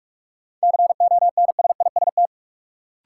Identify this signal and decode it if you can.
Morse code, heard as a keyed tone, about 41 wpm, 700 Hz. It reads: CONSIST
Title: Sentences from Top 1400 Words - New Word Drill Alt - 45wpm